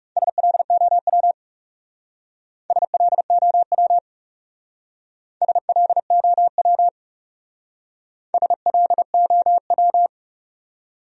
Morse code, heard as a keyed tone, about 37 wpm, 700 Hz. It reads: SLOW SLOW SLOW SLOW
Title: Speed-Racing Top 500 Words 30wpm